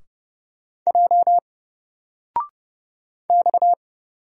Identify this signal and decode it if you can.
Morse code, heard as a keyed tone, about 30 wpm, 700 Hz. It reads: J X
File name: Single-Letter-Number-30wpm